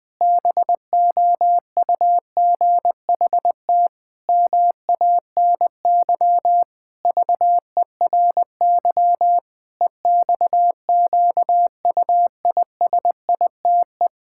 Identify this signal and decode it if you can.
Morse code, heard as a keyed tone, about 20 wpm, 700 Hz. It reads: BOUGHT MANY VERY EXQUISITE
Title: Pangram Challenge v2 - 20wpm